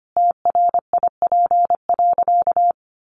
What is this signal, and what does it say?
Morse code, heard as a keyed tone, about 25 wpm, 700 Hz. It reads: TRIP.
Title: Sentences from Top 1500 Words 25wpm